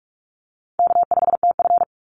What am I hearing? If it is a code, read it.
Morse code, heard as a keyed tone, about 44 wpm, 700 Hz. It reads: K5TF
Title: Easy Call Signs Repeated 2x - 45wpm